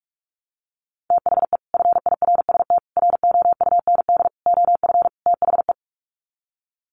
Morse code, heard as a keyed tone, about 44 wpm, 700 Hz. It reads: THE FIRST ROUND OF THE
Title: Sets of 5 Words Encore 45wpm